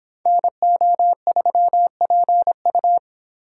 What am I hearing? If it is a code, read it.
Morse code, heard as a keyed tone, about 26 wpm, 700 Hz. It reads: NO3PU